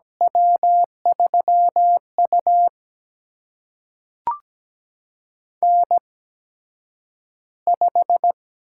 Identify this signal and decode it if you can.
Morse code, heard as a keyed tone, about 17 wpm, spaced as for 9 wpm, 700 Hz. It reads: W3U N5
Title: USA Call Signs - 4 Characters - 8x Character Spacing - 17wpm